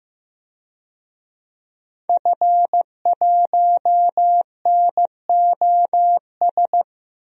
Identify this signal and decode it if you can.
Morse code, heard as a keyed tone, about 15 wpm, 700 Hz. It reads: F1NOS
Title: DXCC-100 Call Signs - 5 Characters - 15wpm